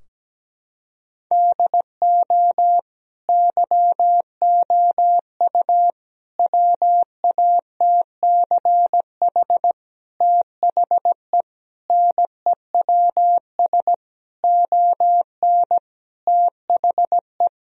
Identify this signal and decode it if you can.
Morse code, heard as a keyed tone, about 17 wpm, 700 Hz. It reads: DO YOU WATCH THE NEWS ON THE
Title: Sentences from Top 1000 Words - Review Part 4 - 17wpm